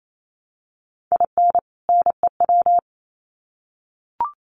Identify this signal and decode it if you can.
Morse code, heard as a keyed tone, about 28 wpm, 700 Hz. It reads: IN NEW
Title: Sets of 2 Words 28wpm